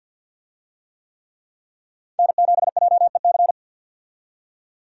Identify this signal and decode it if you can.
Morse code, heard as a keyed tone, about 47 wpm, 700 Hz. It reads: N7JEC
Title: USA Call Signs - 5 Characters - 50wpm